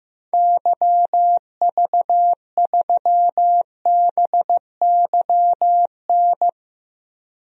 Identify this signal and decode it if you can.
Morse code, heard as a keyed tone, about 15 wpm, 700 Hz. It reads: YV3BYN
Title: DXCC-100 Call Signs - 6 Characters - 15wpm